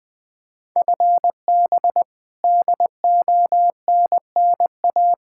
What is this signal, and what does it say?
Morse code, heard as a keyed tone, about 20 wpm, 700 Hz. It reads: FB DONNA
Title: Practice QSO - Exchange 3 - More Info - 20wpm